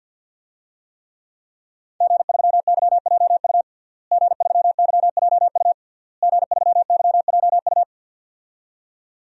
Morse code, heard as a keyed tone, about 50 wpm, 700 Hz. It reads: G3YJU G3YJU G3YJU
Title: DXCC-100 Call Signs - 5 Characters - Repeated 3x - 50wpm